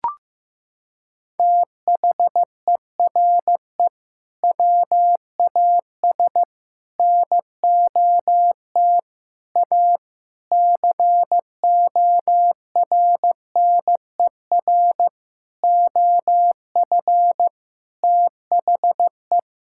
Morse code, heard as a keyed tone, about 15 wpm, 700 Hz. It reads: THERE WAS NOT A CORNER OF THE
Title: Sentences from Top 1600 Words 15wpm